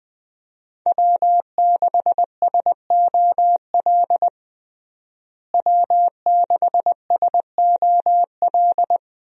Morse code, heard as a keyed tone, about 20 wpm, 700 Hz. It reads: W6SOL W6SOL
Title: USA Call Signs - 5 Characters - Repeated 3x - 20wpm